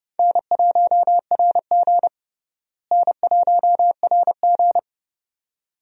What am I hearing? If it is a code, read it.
Morse code, heard as a keyed tone, about 30 wpm, 700 Hz. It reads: N1RG N1RG
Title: USA Call Signs - 4 Characters - Repeated 2x - 30wpm